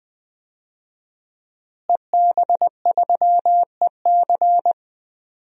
Morse code, heard as a keyed tone, about 21 wpm, 700 Hz. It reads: EB3EC
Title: DXCC-100 Call Signs - 5 Characters - Repeated 2x - 20wpm